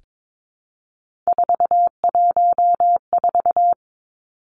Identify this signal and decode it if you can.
Morse code, heard as a keyed tone, about 22 wpm, 700 Hz. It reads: 414
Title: Numbers - 3-Digit 22wpm